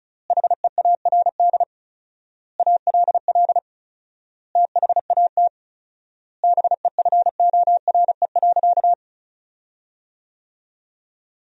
Morse code, heard as a keyed tone, about 36 wpm, 700 Hz. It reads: HEARD ALL THAT BEFORE.
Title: 4x Word-Spacing - Sentences from Top 400 Words - 35wpm